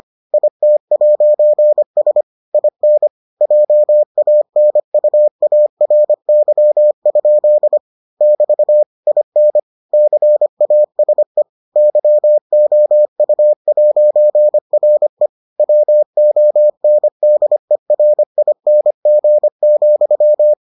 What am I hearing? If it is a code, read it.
Morse code, heard as a keyed tone, about 25 wpm, 600 Hz. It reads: IT'S IN JANUARY? <BT> IN CASE YOU'RE WONDERING,